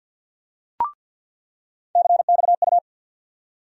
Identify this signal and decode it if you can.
Morse code, heard as a keyed tone, about 46 wpm, 700 Hz. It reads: CXU
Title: USA Call Sign Postfixes - 3 Letters 50wpm